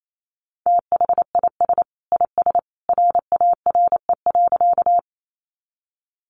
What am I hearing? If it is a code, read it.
Morse code, heard as a keyed tone, about 28 wpm, 700 Hz. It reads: THIS IS RARE.
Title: Sentences from Top 1100 Words 28wpm